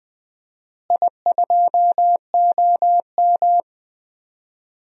Morse code, heard as a keyed tone, about 20 wpm, 700 Hz. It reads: I2OM
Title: DXCC-100 Call Signs - 4 Characters - Repeated 3x - 20wpm